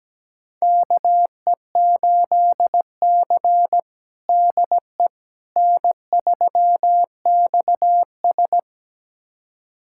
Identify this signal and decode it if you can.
Morse code, heard as a keyed tone, about 17 wpm, 700 Hz. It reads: KE8C DE N3XS
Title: USA Callsign DE Callsign - 4 Characters - 17wpm